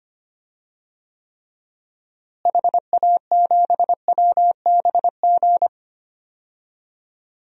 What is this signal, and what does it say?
Morse code, heard as a keyed tone, about 25 wpm, 700 Hz. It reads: HA7WBG
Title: DXCC-100 Call Signs - 6 Characters - Repeated 2x - 25wpm